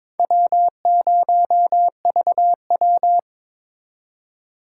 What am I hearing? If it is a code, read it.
Morse code, heard as a keyed tone, about 22 wpm, 700 Hz. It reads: W0VW
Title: USA Call Signs - 2x Character Spacing - 22wpm